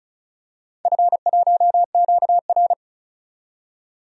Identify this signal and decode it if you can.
Morse code, heard as a keyed tone, about 34 wpm, 700 Hz. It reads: F1QR
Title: DXCC-100 Call Signs - 4 Characters - 35wpm